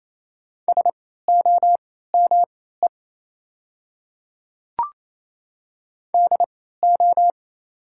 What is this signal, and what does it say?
Morse code, heard as a keyed tone, about 27 wpm, spaced as for 16 wpm, 700 Hz. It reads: SOME DO
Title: 3x Character Spacing - Top 100 Words - 28wpm